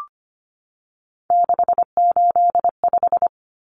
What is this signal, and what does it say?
Morse code, heard as a keyed tone, about 25 wpm, 700 Hz. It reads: 685
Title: Common Cut Numbers - 3-Digit 25wpm